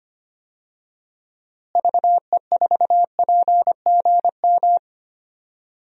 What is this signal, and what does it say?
Morse code, heard as a keyed tone, about 25 wpm, 700 Hz. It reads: VE4PGM
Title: DXCC-100 Call Signs - 6 Characters - Repeated 2x - 25wpm